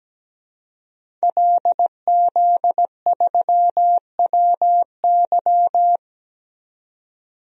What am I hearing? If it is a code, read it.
Morse code, heard as a keyed tone, about 17 wpm, 700 Hz. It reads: LZ3WY
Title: DXCC-100 Call Signs - 5 Characters - 17wpm